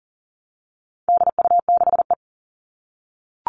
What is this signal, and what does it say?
Morse code, heard as a keyed tone, about 40 wpm, 700 Hz. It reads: DU6E
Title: All Cut Numbers - 4-Digit 40wpm